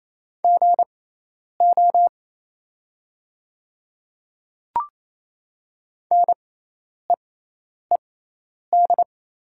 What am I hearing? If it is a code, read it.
Morse code, heard as a keyed tone, about 27 wpm, spaced as for 10 wpm, 700 Hz. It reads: GO NEED
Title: 6x Character Spacing - Top 300 Words - 28wpm